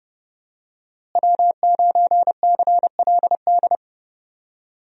Morse code, heard as a keyed tone, about 32 wpm, 700 Hz. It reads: W9CLD
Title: USA Call Signs - 5 Characters - Repeated 2x - 30wpm